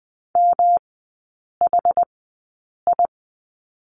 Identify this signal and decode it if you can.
Morse code, heard as a keyed tone, about 20 wpm, spaced as for 8 wpm, 700 Hz. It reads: MHI
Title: 107 Lesson 18e - Letter m - Three Characters 20wpm